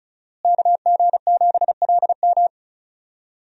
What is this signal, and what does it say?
Morse code, heard as a keyed tone, about 34 wpm, 700 Hz. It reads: KG7LM
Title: USA Call Signs - 5 Characters - Repeated 2x - 35wpm